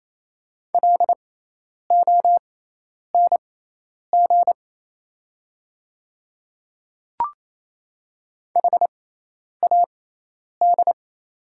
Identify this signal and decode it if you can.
Morse code, heard as a keyed tone, about 27 wpm, spaced as for 10 wpm, 700 Hz. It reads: LONG HAD
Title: 6x Character Spacing - Top 100 Words - 28wpm